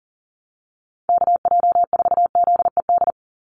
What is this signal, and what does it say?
Morse code, heard as a keyed tone, about 40 wpm, 700 Hz. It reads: KJ4ZED
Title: USA Call Signs 40wpm